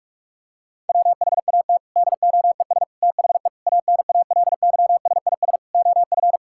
Morse code, heard as a keyed tone, about 45 wpm, 700 Hz. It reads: WHAT DOES THE ANALYSIS OF
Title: Sentences from Top 1000 Words - Review Part 6 - 45wpm